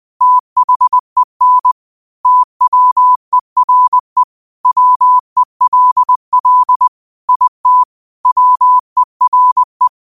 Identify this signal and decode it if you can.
Morse code, heard as a keyed tone, about 20 wpm, 1000 Hz. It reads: THEN TWERE WELL IT WERE